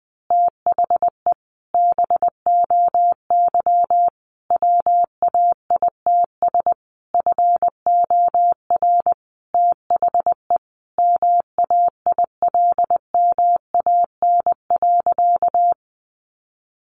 Morse code, heard as a keyed tone, about 20 wpm, 700 Hz. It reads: THE BOY WAITS FOR THE MAILMAN.
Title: Easy Phrases 20wpm